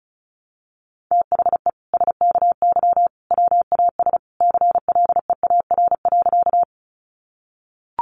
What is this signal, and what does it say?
Morse code, heard as a keyed tone, about 34 wpm, 700 Hz. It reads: THE SKY WAS CLEAR.
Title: Sentences from Top 1200 Words 35wpm